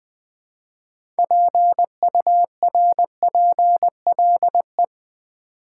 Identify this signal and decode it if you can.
Morse code, heard as a keyed tone, about 20 wpm, 700 Hz. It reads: PURPLE
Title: Six Letter Words - 20wpm